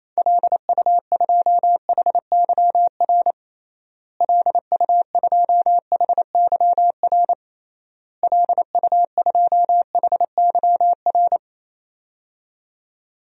Morse code, heard as a keyed tone, about 28 wpm, 700 Hz. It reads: LU2HYR LU2HYR LU2HYR
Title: DXCC-100 Call Signs - 6 Characters - Repeated 3x - 28wpm